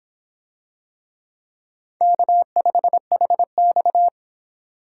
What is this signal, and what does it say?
Morse code, heard as a keyed tone, about 22 wpm, 700 Hz. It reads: K5HX